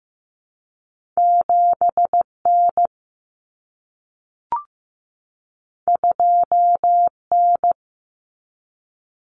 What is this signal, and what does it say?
Morse code, heard as a keyed tone, about 15 wpm, 700 Hz. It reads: 7N 2N
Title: Common Cut Numbers - 2-Digit 15wpm